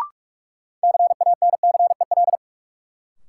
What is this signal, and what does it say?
Morse code, heard as a keyed tone, about 46 wpm, 700 Hz. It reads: CANCEL
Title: Sentences from Top 1200 Words - New Word Drill - 45wpm